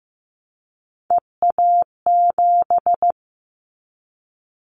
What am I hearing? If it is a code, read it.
Morse code, heard as a keyed tone, about 15 wpm, 700 Hz. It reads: EA7
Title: Callsign Prefixes for Top 25 DXCC - Repeated 3x - 15wpm